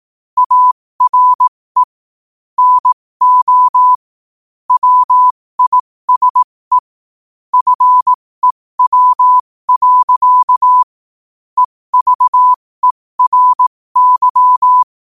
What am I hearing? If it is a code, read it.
Morse code, heard as a keyed tone, about 18 wpm, spaced as for 15 wpm, 1000 Hz. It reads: ARE NO WISE FEW. EVERY